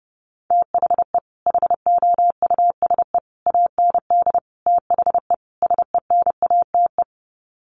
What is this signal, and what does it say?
Morse code, heard as a keyed tone, about 28 wpm, 700 Hz. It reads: THE HOUSE AND THE SENATE
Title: Sets of 5 Words Encore2 30wpm